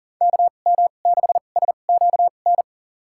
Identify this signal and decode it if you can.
Morse code, heard as a keyed tone, about 40 wpm, spaced as for 29 wpm, 700 Hz. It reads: KM6SQN
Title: USA Call Signs - 2x Character Spacing - 40wpm